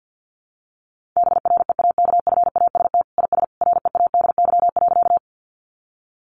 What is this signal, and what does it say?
Morse code, heard as a keyed tone, about 50 wpm, 700 Hz. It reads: BREAKFAST IS READY.
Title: Sentences from Top 1500 Words 50wpm